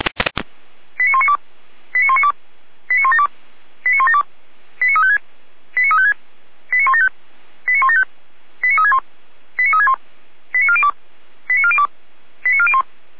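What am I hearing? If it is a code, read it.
Five-tone selective call (ZVEI), heard as quick runs of five tones, tones of about 70 ms. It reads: 89192 89192 89172 89172 89356 89356 89175 89175 89361 89361 89402 89402 89401
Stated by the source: ZVEI five-tone selective call code 89075, received by SDR.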